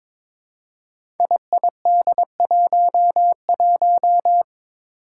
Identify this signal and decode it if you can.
Morse code, heard as a keyed tone, about 22 wpm, 700 Hz. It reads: IID11
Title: Five Challenging Letter-Number Combinations 22wpm